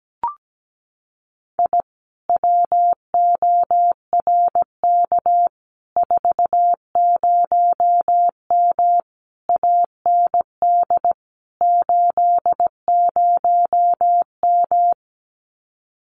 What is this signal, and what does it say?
Morse code, heard as a keyed tone, about 17 wpm, 700 Hz. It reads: I WORK 40M AND 80M
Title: Common QSO Phrases Encore 17wpm